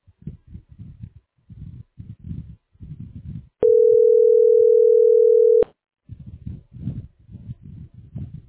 Ringback tone (North America), 440 + 480 Hz.